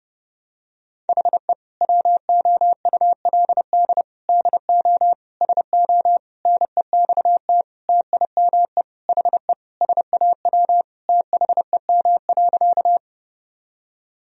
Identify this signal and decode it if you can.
Morse code, heard as a keyed tone, about 32 wpm, 700 Hz. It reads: HE WOULD DO SO NEXT TIME HE SAW THEM.
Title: Sentences from Top 1000 Words - Review Part 1 - 30wpm